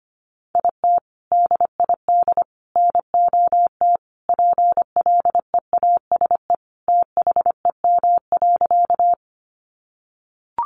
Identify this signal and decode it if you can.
Morse code, heard as a keyed tone, about 25 wpm, 700 Hz. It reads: IT DID NOT PLEASE THEM.
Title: Sentences from Top 1500 Words 25wpm